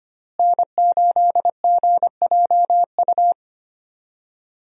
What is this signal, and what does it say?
Morse code, heard as a keyed tone, about 25 wpm, 700 Hz. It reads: N8GJU
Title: USA Call Signs - 5 Characters - Repeated 2x - 25wpm